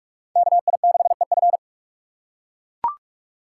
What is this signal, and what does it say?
Morse code, heard as a keyed tone, about 44 wpm, 700 Hz. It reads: KI6EF